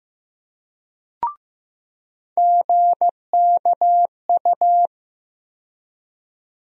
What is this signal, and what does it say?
Morse code, heard as a keyed tone, about 15 wpm, 700 Hz. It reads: GKU
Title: USA Call Sign Postfixes - 3 Letters 15wpm